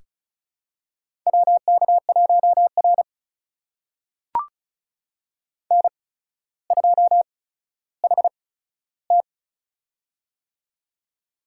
Morse code, heard as a keyed tone, about 36 wpm, spaced as for 15 wpm, 700 Hz. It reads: WK1R N2HT
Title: USA Call Signs - 4 Characters - 8x Character Spacing - 35wpm